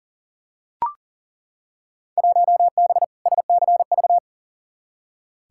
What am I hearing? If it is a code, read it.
Morse code, heard as a keyed tone, about 40 wpm, 700 Hz. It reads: 1B SCV
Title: Field Day Exchanges Encore - 40wpm